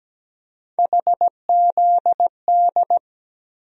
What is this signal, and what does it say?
Morse code, heard as a keyed tone, about 17 wpm, 700 Hz. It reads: HZD
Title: USA Call Sign Postfixes - 3 Letters 17wpm